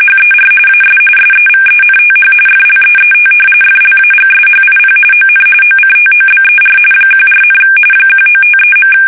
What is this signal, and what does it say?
RTTY, about 45.45 baud. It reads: HER AS SEVENTH FLEET FLAGSHIP ON OCTOBER 14, 1953.
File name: fsk850 / battleshipnj